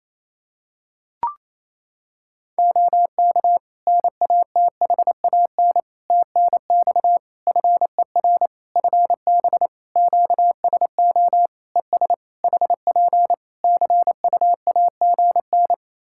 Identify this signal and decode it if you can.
Morse code, heard as a keyed tone, about 28 wpm, 700 Hz. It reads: OK NATHAN TNX FER FB QSO ES HP CUAGN
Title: Practice QSO - Exchange 4 - Ending - 28wpm